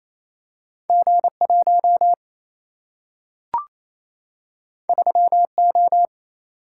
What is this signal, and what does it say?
Morse code, heard as a keyed tone, about 28 wpm, 700 Hz. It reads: G1 3O
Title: Single-Letter Single-Number 28wpm